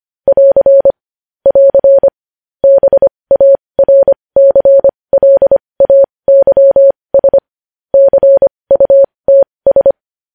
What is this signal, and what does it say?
Morse code, heard as a keyed tone, about 25 wpm, spaced as for 20 wpm, 550 Hz. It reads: <AR> <AR> BARCLAYS CUTS
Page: news headlines in morse code at 20 wpm